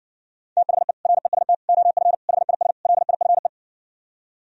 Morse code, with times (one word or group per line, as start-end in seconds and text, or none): 0.6-0.9 THE
1.1-1.5 REST
1.7-2.1 OF
2.3-2.7 HIS
2.9-3.5 LIFE